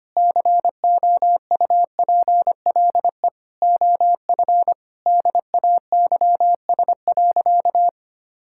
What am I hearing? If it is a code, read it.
Morse code, heard as a keyed tone, about 25 wpm, 700 Hz. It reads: COUPLE OF DAYS.
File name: practice / 25wpm / Sentences with 4-Word Phrases - 25wpm